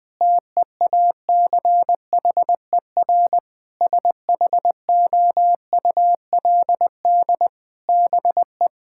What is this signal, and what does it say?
Morse code, heard as a keyed tone, about 20 wpm, 700 Hz. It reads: TEACHER SHOULD BE
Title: Sentences from Top 1000 Words - Review Part 6 - 20wpm